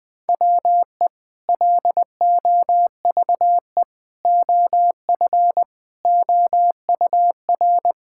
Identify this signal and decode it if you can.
Morse code, heard as a keyed tone, about 21 wpm, 700 Hz. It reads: WE LOVE OF OUR
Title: Sentences with 2-Word Phrases - 20wpm